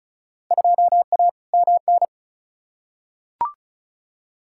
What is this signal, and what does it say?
Morse code, heard as a keyed tone, about 34 wpm, 700 Hz. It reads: 2A MN
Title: Field Day Exchanges - 35wpm